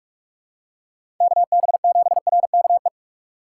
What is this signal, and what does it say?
Morse code, heard as a keyed tone, about 45 wpm, 700 Hz. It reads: KB7RKE
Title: USA Call Signs - 6 Characters - 3x Character Spacing - 45wpm